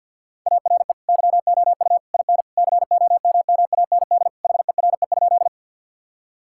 Morse code, heard as a keyed tone, about 50 wpm, 700 Hz. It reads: ARE YOU IN COMMAND HERE?
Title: Sentences from Top 1600 Words 50wpm